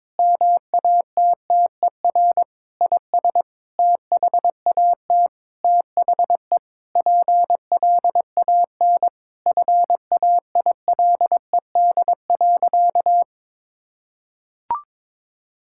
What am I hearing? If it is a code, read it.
Morse code, heard as a keyed tone, about 22 wpm, 700 Hz. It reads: MATTER IS THAT THE PLAN FAILED.
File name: Sentences with 9-Word Phrases - 22wpm